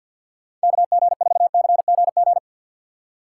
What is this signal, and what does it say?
Morse code, heard as a keyed tone, about 51 wpm, 700 Hz. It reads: KG4CGG
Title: USA Call Signs - 6 Characters - 6x Character Spacing - 50wpm